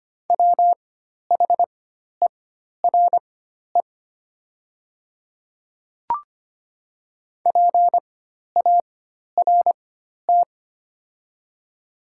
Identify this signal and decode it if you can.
Morse code, heard as a keyed tone, about 25 wpm, spaced as for 12 wpm, 690 Hz. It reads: WHERE PART
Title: 4x Character Spacing - Top 200 Words - 25wpm